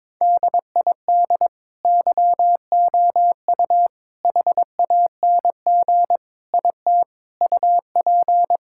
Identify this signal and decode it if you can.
Morse code, heard as a keyed tone, about 22 wpm, 700 Hz. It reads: DID YOU HANG IT UP
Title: Sentences from Top 1600 Words 22wpm